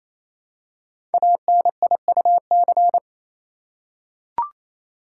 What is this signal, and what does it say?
Morse code, heard as a keyed tone, about 27 wpm, 700 Hz. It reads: ANIUC
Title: Code Groups - 5 Letter:Numbers - 28wpm